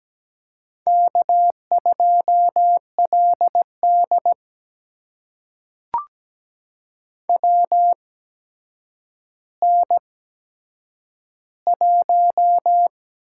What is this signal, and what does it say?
Morse code, heard as a keyed tone, about 17 wpm, spaced as for 8 wpm, 700 Hz. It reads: K2LD WN1
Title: USA Call Signs - 8x Character Spacing - 17wpm